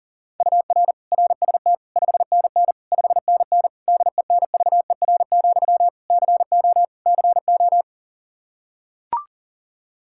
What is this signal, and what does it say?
Morse code, heard as a keyed tone, about 40 wpm, 700 Hz. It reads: UR RST 5NN 5NN DENVER, CO CO
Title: Practice QSO - Exchange 1 - Beginning - 40wpm